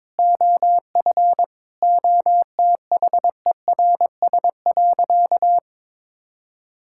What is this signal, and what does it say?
Morse code, heard as a keyed tone, about 22 wpm, 700 Hz. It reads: OF OTHERS.
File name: practice / 22wpm / Sentences with 4-Word Phrases - 22wpm